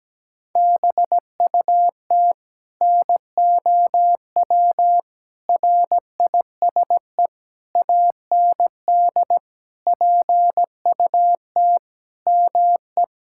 Morse code, heard as a keyed tone, about 17 wpm, 700 Hz. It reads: BUT NOW RISE AND PUT ME
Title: Sentences from Top 1000 Words - Review Part 6 - 17wpm